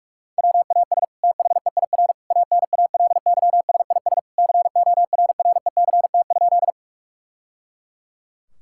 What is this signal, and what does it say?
Morse code, heard as a keyed tone, about 47 wpm, 700 Hz. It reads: WAS THEIR ANALYSIS CORRECT?
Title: Sentences from Top 1000 Words - Review Part 5 - 45wpm